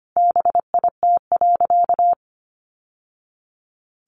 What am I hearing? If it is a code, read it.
Morse code, heard as a keyed tone, about 25 wpm, 700 Hz. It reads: BIT.
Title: Sentences from Top 900 Words 25wpm